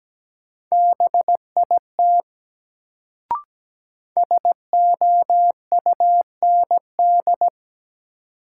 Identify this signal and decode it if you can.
Morse code, heard as a keyed tone, about 17 wpm, 700 Hz. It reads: BIT SOUND
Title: Sets of 3 Words - New Word Drill - 17wpm